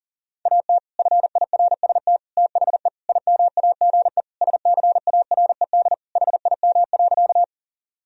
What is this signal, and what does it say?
Morse code, heard as a keyed tone, about 37 wpm, 700 Hz. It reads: AT FIRST THE IMAGE SCARED HIM.
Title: Sentences from Top 1000 Words - Review Part 4 - 40wpm